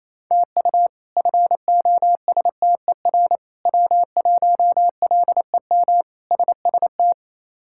Morse code, heard as a keyed tone, about 27 wpm, 700 Hz. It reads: TU FOSTER W1LEM SST
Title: K1USN SST QSO Practice - 28wpm